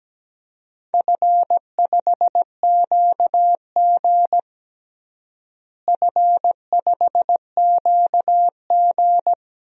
Morse code, heard as a keyed tone, about 17 wpm, 700 Hz. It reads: F5QG F5QG
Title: DXCC-100 Call Signs - 4 Characters - Repeated 2x - 17wpm